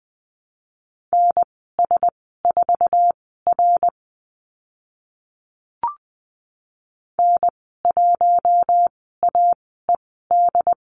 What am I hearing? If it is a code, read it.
Morse code, heard as a keyed tone, about 20 wpm, spaced as for 14 wpm, 700 Hz. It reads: NS4R N1AED
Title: 076 Lesson 13g - Letter L - Call Signs 20wpm